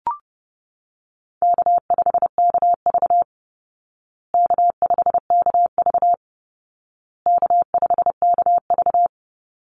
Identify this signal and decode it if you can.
Morse code, heard as a keyed tone, about 30 wpm, 700 Hz. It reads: K5KV K5KV K5KV
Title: Call Signs Repeated 2x - 30wpm